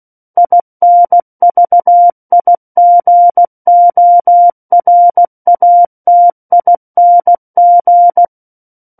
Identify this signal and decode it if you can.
Morse code, heard as a keyed tone, about 16 wpm, 700 Hz. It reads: INVIGORATING
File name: Book One / 07-Chapter 7 - 16 WPM